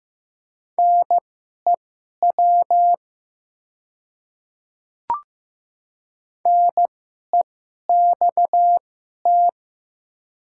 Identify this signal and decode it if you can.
Morse code, heard as a keyed tone, about 15 wpm, spaced as for 11 wpm, 700 Hz. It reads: NEW NEXT
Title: 2x Character Spacing - Top 300 Words - 15wpm